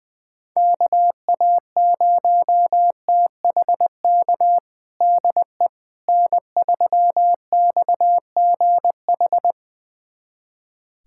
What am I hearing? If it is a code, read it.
Morse code, heard as a keyed tone, about 20 wpm, 700 Hz. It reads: KA0THK DE N3XGH
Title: USA Callsign DE Callsign 20wpm